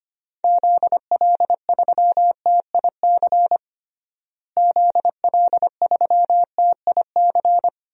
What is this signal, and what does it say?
Morse code, heard as a keyed tone, about 25 wpm, 700 Hz. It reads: ZL3TIC ZL3TIC
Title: DXCC-100 Call Signs - 6 Characters - Repeated 2x - 25wpm